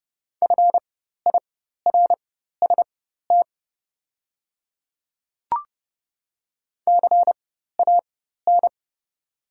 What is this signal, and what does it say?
Morse code, heard as a keyed tone, about 30 wpm, spaced as for 14 wpm, 690 Hz. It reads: FIRST CAN
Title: 4x Character Spacing - Top 100 Words - 30wpm